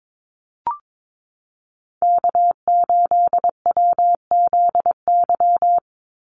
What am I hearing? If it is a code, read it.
Morse code, heard as a keyed tone, about 22 wpm, 700 Hz. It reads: K8WZY
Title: USA Call Signs 22wpm